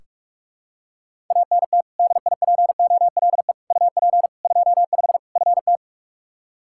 Morse code, heard as a keyed tone, about 42 wpm, 700 Hz. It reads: ANT DIPOLE UP 25 FT
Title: Practice QSO - Exchange 2 - Info - 45wpm